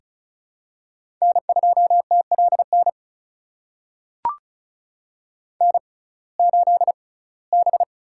Morse code, heard as a keyed tone, about 36 wpm, spaced as for 21 wpm, 700 Hz. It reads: N2TLN N8B
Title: USA Call Signs - 5 Characters - 6x Character Spacing - 35wpm